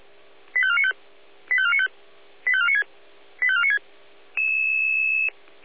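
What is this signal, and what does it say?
Five-tone selective call (ZVEI), heard as quick runs of five tones, tones of about 70 ms. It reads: 85495 85495 85496 85496
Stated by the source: ZVEI five-tone call received by SDR.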